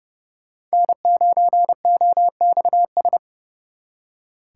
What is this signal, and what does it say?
Morse code, heard as a keyed tone, about 34 wpm, 700 Hz. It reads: N9OXS
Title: Speed Racing - USA 1x3 Call Signs 30wpm